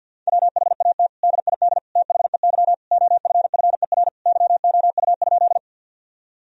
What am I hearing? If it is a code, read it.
Morse code, heard as a keyed tone, about 50 wpm, 700 Hz. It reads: WHAT DID THEY OFFER YOU?
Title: Sentences from Top 1000 Words - Review Part 4 - 50wpm